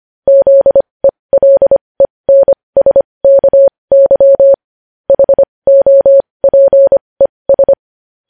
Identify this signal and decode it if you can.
Morse code, heard as a keyed tone, about 25 wpm, spaced as for 20 wpm, 550 Hz. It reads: ZELENSKY HOPES
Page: News Headlines in Morse Code at 20 WPM